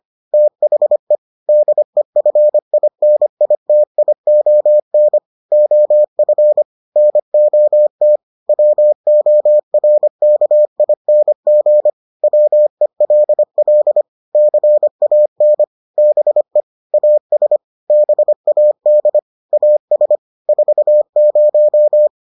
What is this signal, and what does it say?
Morse code, heard as a keyed tone, about 25 wpm, 600 Hz. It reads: THE DEFINITION OF NOT WORKING WELL CAN BE AS BAD AS 40